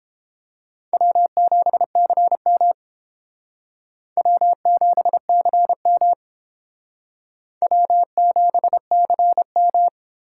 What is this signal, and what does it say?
Morse code, heard as a keyed tone, about 32 wpm, 700 Hz. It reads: W7CM W7CM W7CM